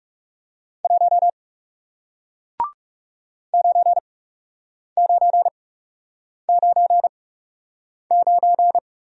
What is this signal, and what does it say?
Morse code, heard as a keyed tone, about 40 wpm, spaced as for 8 wpm, 700 Hz. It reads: 1 9999